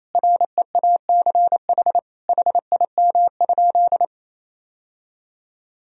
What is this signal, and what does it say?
Morse code, heard as a keyed tone, about 27 wpm, 700 Hz. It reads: REACH HIM?
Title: Sentences from Top 1000 Words - Review Part 2 - 28wpm